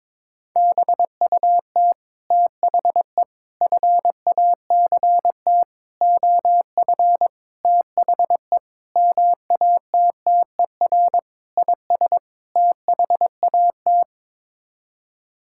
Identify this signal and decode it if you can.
Morse code, heard as a keyed tone, about 22 wpm, 700 Hz. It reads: BUT THE FACT OF THE MATTER IS THAT
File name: practice / 22wpm / Sets of 8 Words - 22wpm